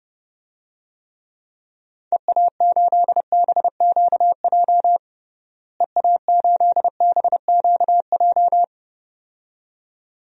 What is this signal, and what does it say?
Morse code, heard as a keyed tone, about 30 wpm, 700 Hz. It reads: EA8BQJ EA8BQJ
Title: DXCC-100 Call Signs - 6 Characters - Repeated 2x - 30wpm